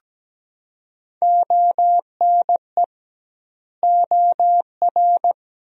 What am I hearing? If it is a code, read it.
Morse code, heard as a keyed tone, about 17 wpm, 700 Hz. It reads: ONE OR
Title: 2x Word-Spacing - Sentences from Top 100 Words - 17wpm